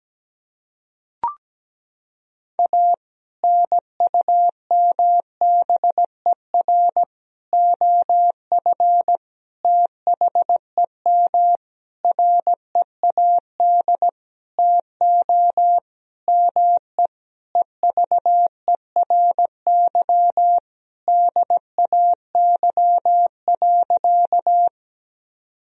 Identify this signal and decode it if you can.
Morse code, heard as a keyed tone, about 17 wpm, 700 Hz. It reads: A NUMBER OF THEM READ TO ME EVERY DAY.
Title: Sentences from Top 1000 Words - Review Part 1 - 17wpm